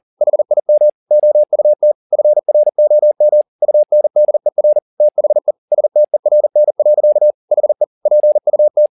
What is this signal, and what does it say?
Morse code, heard as a keyed tone, about 40 wpm, 600 Hz. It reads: HIM OUT FROM UNDER THE STERN. HE PUT
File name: the old man and the sea 2 40wpm